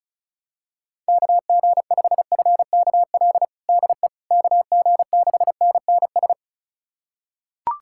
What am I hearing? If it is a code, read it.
Morse code, heard as a keyed tone, about 36 wpm, 700 Hz. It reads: KG5FKL DE KG6NNS
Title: USA Callsign DE Callsign 35wpm